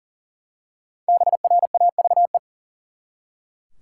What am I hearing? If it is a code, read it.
Morse code, heard as a keyed tone, about 37 wpm, 700 Hz. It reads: BRAVE
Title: Five Letter Words - 40wpm